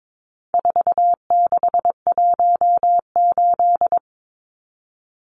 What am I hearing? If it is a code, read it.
Morse code, heard as a keyed tone, about 22 wpm, 700 Hz. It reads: 4618
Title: Numbers - 4-Digit 22wpm